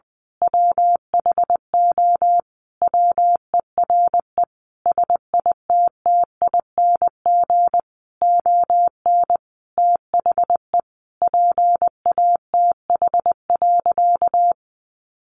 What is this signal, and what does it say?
Morse code, heard as a keyed tone, about 20 wpm, 700 Hz. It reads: WHO WERE SITTING ON THE PATH.
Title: A Sound Of Thunder 20wpm